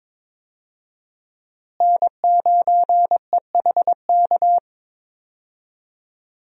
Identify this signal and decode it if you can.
Morse code, heard as a keyed tone, about 22 wpm, 700 Hz. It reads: N9EHK
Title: USA Call Signs - 5 Characters - Repeated 3x - 22wpm